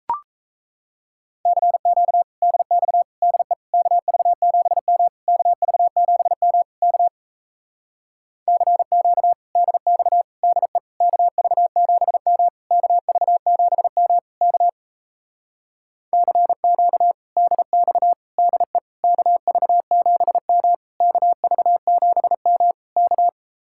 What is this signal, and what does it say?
Morse code, heard as a keyed tone, about 38 wpm, 700 Hz. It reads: CQ DX DE KV7M KV7M K CQ DX DE KV7M KV7M K CQ DX DE KV7M KV7M K